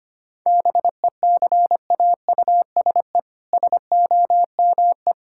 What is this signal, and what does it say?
Morse code, heard as a keyed tone, about 25 wpm, 700 Hz. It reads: BECAUSE SOME
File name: practice / 25wpm / All-Age Appropriate Jokes - 25wpm